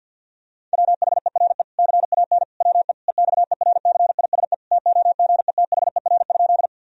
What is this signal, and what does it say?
Morse code, heard as a keyed tone, about 50 wpm, 700 Hz. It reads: WHERE CAN WE EXERCISE TOGETHER?
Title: Sentences from Top 1000 Words - Review Part 4 - 50wpm